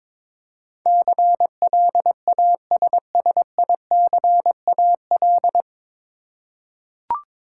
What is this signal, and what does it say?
Morse code, heard as a keyed tone, about 21 wpm, 700 Hz. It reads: CLASSICAL
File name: 9 Letter Words - First 3 Letters Given - 22wpm